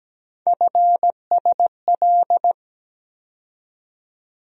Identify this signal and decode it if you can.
Morse code, heard as a keyed tone, about 17 wpm, 700 Hz. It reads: FSL
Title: USA Call Sign Postfixes - 3 Letters 17wpm